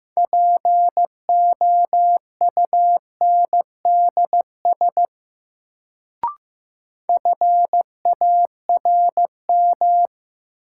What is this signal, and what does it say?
Morse code, heard as a keyed tone, about 15 wpm, 700 Hz. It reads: POUNDS FARM
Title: Sentences from Top 1000 Words - Review Part 2 - New Word Drill - 15wpm